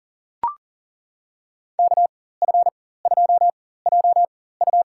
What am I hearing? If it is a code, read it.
Morse code, heard as a keyed tone, about 40 wpm, spaced as for 19 wpm, 700 Hz. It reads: KF2JU
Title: USA Call Signs - 5 Characters - 4x Character Spacing - 40wpm